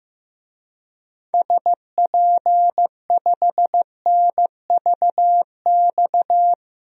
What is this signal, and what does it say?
Morse code, heard as a keyed tone, about 16 wpm, 700 Hz. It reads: SP5NVX
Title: DXCC-100 Call Signs - 6 Characters - 15wpm